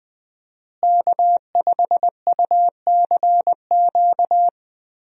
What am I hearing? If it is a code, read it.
Morse code, heard as a keyed tone, about 20 wpm, 700 Hz. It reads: K5UCQ
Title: USA Call Signs - 5 Characters - Repeated 3x - 20wpm